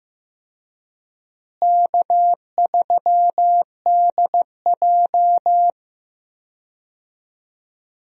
Morse code, heard as a keyed tone, about 15 wpm, 700 Hz. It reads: K3DJ
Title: USA Call Signs - 4 Characters - Repeated 2x - 15wpm